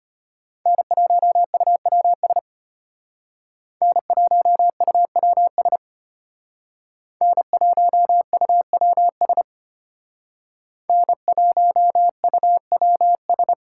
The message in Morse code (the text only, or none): N1UWS N1UWS N1UWS N1UWS